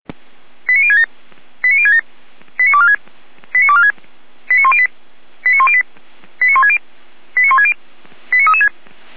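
Five-tone selective call (ZVEI), heard as quick runs of five tones, tones of about 70 ms. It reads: 89067 89067 89246 89246 89108 89108 89160 89160 89336
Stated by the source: Five-tone ZVEI selective call received by SDR, archived by feuerwehr-orlamuende (fire brigade Orlamünde).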